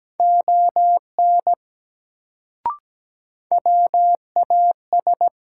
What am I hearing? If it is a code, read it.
Morse code, heard as a keyed tone, about 17 wpm, 700 Hz. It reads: ON WAS
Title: Sentences from Top 100 Words - New Word Drill - 17wpm